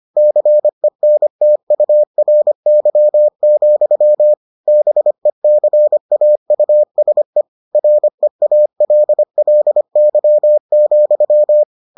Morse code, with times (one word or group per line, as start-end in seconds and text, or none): 0.2-4.3 CENTURY,
4.7-7.4 BECAUSE
7.8-11.6 REALLY,